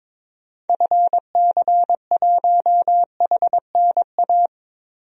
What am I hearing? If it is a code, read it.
Morse code, heard as a keyed tone, about 22 wpm, 700 Hz. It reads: FC1HNA
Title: DXCC-100 Call Signs - 6 Characters - 22wpm